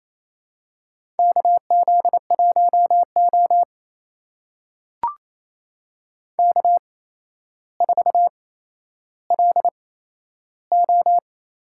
Morse code, heard as a keyed tone, about 28 wpm, spaced as for 12 wpm, 700 Hz. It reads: KZ1O K4LO
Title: USA Call Signs - 4 Characters - 8x Character Spacing - 28wpm